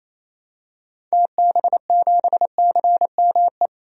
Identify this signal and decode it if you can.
Morse code, heard as a keyed tone, about 27 wpm, 700 Hz. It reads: TB7CME
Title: DXCC-100 Call Signs - 6 Characters - 28wpm